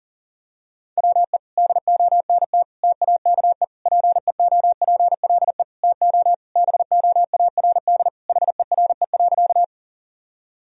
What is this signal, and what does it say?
Morse code, heard as a keyed tone, about 40 wpm, 700 Hz. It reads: WE DONT TAKE PEOPLE TO BOARD HERE.
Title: Sentences from Top 1000 Words - Review Part 4 - 40wpm